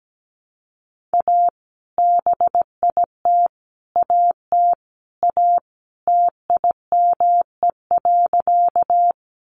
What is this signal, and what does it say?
Morse code, heard as a keyed tone, about 17 wpm, 700 Hz. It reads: A BIT AT A TIME.
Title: Sentences from Top 900 Words 17wpm